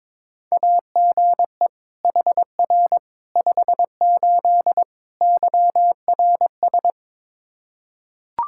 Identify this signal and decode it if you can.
Morse code, heard as a keyed tone, about 22 wpm, 700 Hz. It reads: AGE HR 58 YRS
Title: Practice QSO - Exchange 3 - More Info - 22wpm